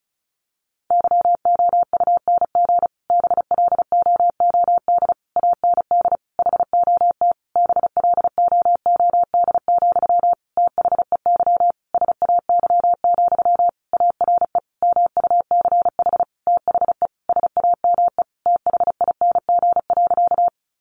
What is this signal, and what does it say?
Morse code, heard as a keyed tone, about 34 wpm, 700 Hz. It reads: YOUNG BLOOD AND HOT BLOOD, THEY SAY, ARE MUCH THE SAME THING.
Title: Sentences from Top 1100 Words 35wpm